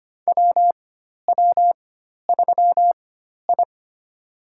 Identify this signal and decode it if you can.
Morse code, heard as a keyed tone, about 25 wpm, spaced as for 12 wpm, 700 Hz. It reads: WW3I
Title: USA Call Signs - 4 Characters - 4x Character Spacing - 25wpm